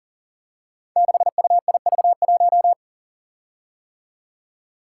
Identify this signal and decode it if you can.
Morse code, heard as a keyed tone, about 40 wpm, 700 Hz. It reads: 6UIV1
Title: Code Groups - 5 Letter:Numbers - 40wpm